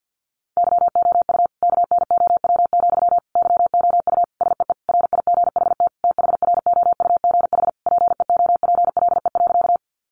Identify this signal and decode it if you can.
Morse code, heard as a keyed tone, about 48 wpm, 700 Hz. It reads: YOU KNOW, YOU SEE RIGHT THROUGH PEOPLE.
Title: Sentences from Top 200 Words 50wpm